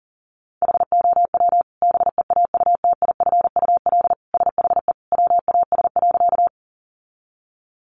Morse code, heard as a keyed tone, about 40 wpm, 700 Hz. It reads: HOW BEAUTIFUL SHE WAS.
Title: Sentences from Top 900 Words 40wpm